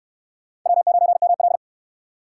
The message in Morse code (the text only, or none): W9ML